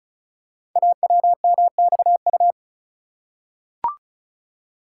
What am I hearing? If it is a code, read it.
Morse code, heard as a keyed tone, about 36 wpm, 690 Hz. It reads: AWMXU
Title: Code Groups - 5 Letters - 35wpm